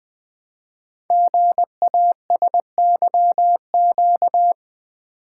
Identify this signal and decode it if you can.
Morse code, heard as a keyed tone, about 20 wpm, 700 Hz. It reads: GASYQ
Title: Code Groups - 5 Letter:Numbers - 20wpm